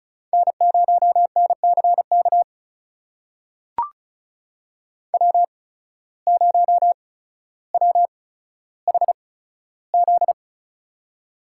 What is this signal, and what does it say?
Morse code, heard as a keyed tone, about 34 wpm, spaced as for 15 wpm, 700 Hz. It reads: N0NCK W0WHZ
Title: USA Call Signs - 5 Characters - 8x Character Spacing - 35wpm